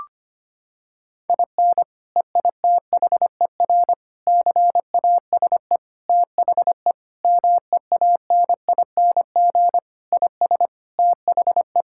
Morse code, heard as a keyed tone, about 25 wpm, 700 Hz. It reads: IN EITHER CASE THE MEANING IS THE
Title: Sentences from Top 1000 Words - Review Part 6 - 25wpm